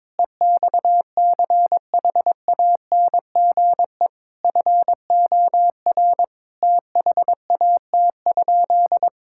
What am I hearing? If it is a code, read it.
Morse code, heard as a keyed tone, about 22 wpm, 700 Hz. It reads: EXCHANGE FOR THAT?
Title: Sentences from Top 1000 Words - Review Part 6 - 22wpm